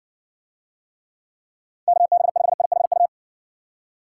Morse code, heard as a keyed tone, about 60 wpm, 700 Hz. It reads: KB5IHU